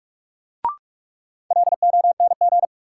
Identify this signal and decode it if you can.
Morse code, heard as a keyed tone, about 46 wpm, 700 Hz. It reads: LONG
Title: Sentences from Top 100 Words - New Word Drill - 45wpm